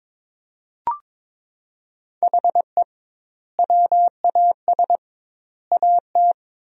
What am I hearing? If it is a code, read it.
Morse code, heard as a keyed tone, about 22 wpm, 700 Hz. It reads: HE WAS AT
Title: 2x Word-Spacing - Sentences from Top 200 Words - 22wpm